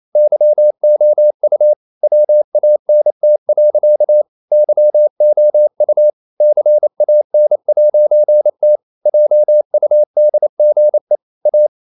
Morse code, heard as a keyed tone, about 28 wpm, 600 Hz. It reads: YOU WANT. YOU CAN'T JUDGE A
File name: CW / Practice / EnglishSayings28